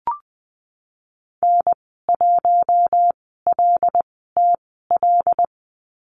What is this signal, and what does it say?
Morse code, heard as a keyed tone, about 20 wpm, spaced as for 15 wpm, 700 Hz. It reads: N1LTL
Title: 076 Lesson 13g - Letter L - Call Signs 20wpm